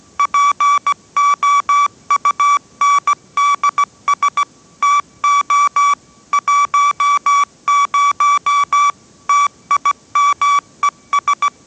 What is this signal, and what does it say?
Morse code, heard as a keyed tone, about 17 wpm, 1150 Hz. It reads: POUNDS TO 10 TIMES